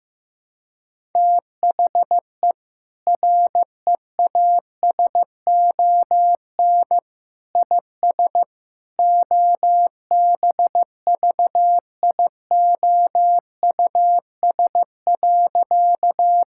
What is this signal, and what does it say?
Morse code, heard as a keyed tone, about 15 wpm, 700 Hz. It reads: THE REASON IS OBVIOUS.
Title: Sentences from Top 1600 Words 15wpm